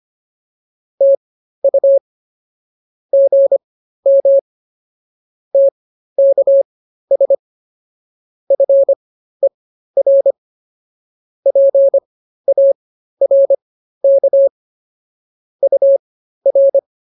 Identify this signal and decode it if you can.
Morse code, heard as a keyed tone, about 24 wpm, spaced as for 13 wpm, 550 Hz. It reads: TU GM TKS FER PARK UR